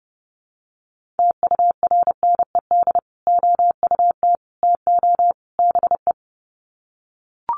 Morse code, heard as a keyed tone, about 30 wpm, 700 Hz. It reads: TURNED OUT TO BE
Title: Sets of 4 Words 30wpm